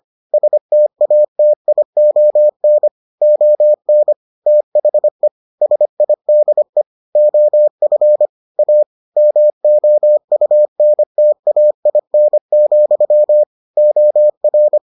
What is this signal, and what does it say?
Morse code, heard as a keyed tone, about 25 wpm, 600 Hz. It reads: STATION ON THE SIDE OF A MOUNTAIN, OR